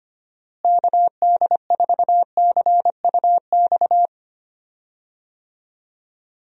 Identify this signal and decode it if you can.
Morse code, heard as a keyed tone, about 25 wpm, 700 Hz. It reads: KD4CUX
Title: USA Call Signs - 6 Characters - Repeated 2x - 25wpm